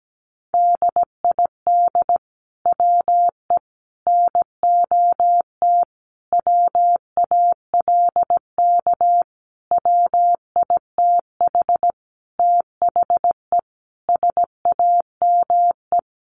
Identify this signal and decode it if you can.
Morse code, heard as a keyed tone, about 17 wpm, 700 Hz. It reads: DID WE NOT WALK WITH THE SAME
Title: Sentences from Top 1400 Words 17wpm